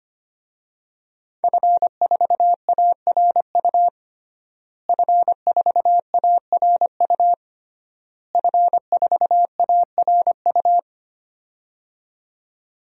Morse code, heard as a keyed tone, about 25 wpm, 700 Hz. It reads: F4ARU F4ARU F4ARU
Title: DXCC-100 Call Signs - 5 Characters - Repeated 3x - 25wpm